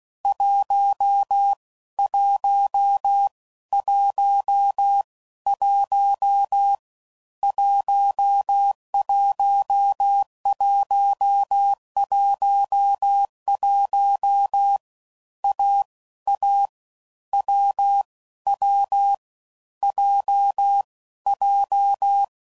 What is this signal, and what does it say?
Morse code, heard as a keyed tone, about 16 wpm, spaced as for 13 wpm, 800 Hz. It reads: 1111 11111 AA WW JJ
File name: Combinations / Difficult Combos / AWJ1